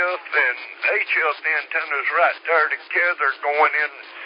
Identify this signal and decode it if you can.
Emergency Alert System (SAME) tones, heard as harsh buzzing bursts, with speech. It is heard from 0.2 to 3.1 s.